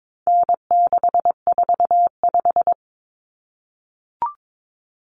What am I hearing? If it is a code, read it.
Morse code, heard as a keyed tone, about 22 wpm, 700 Hz. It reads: N645
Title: Common Cut Numbers - 4-Digit 22wpm